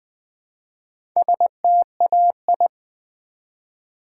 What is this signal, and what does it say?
Morse code, heard as a keyed tone, about 20 wpm, 700 Hz. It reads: STAI
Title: Build A Word - 1 Letter At A Time - 5-Letter Words - 20wpm